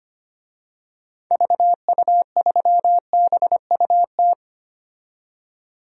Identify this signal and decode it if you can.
Morse code, heard as a keyed tone, about 25 wpm, 700 Hz. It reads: VU3BUT
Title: DXCC-100 Call Signs - 6 Characters - 25wpm